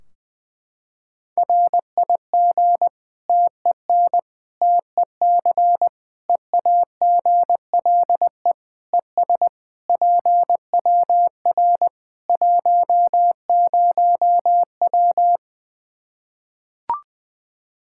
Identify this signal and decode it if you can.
Morse code, heard as a keyed tone, about 21 wpm, 700 Hz. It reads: RIG TEN TEC EAGLE ES PWR 10W
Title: K1USN SST QSO Practice 20wpm